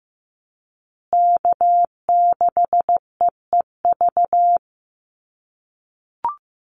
Morse code, heard as a keyed tone, about 15 wpm, 700 Hz. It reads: K6EEV
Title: USA Call Signs Repeated 2x 15wpm